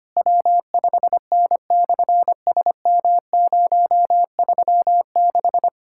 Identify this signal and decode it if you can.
Morse code, heard as a keyed tone, about 27 wpm, 700 Hz. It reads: W5N/SM036
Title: SOTA QSO Practice 25wpm